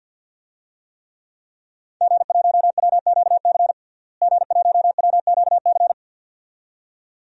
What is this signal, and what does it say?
Morse code, heard as a keyed tone, about 50 wpm, 700 Hz. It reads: G1WQC G1WQC
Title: DXCC-100 Call Signs - 5 Characters - Repeated 2x - 50wpm